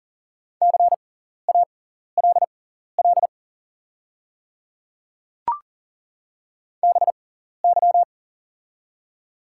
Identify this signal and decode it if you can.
Morse code, heard as a keyed tone, about 42 wpm, spaced as for 14 wpm, 700 Hz. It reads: CALL BY
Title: 6x Character Spacing - Top 100 Words - 40wpm